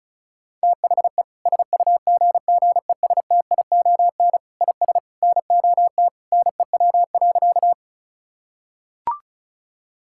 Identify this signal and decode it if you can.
Morse code, heard as a keyed tone, about 34 wpm, 700 Hz. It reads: THE SUGGESTION IS NOT NEW.
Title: Sentences from Top 1600 Words 35wpm